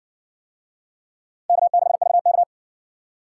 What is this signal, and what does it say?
Morse code, heard as a keyed tone, about 60 wpm, 700 Hz. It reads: K6VK